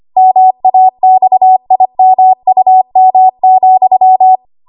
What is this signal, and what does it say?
Morse code, heard as a keyed tone, about 25 wpm, 750 Hz. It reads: MAXIMUM,
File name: Archive / 25 WPM / 25WPM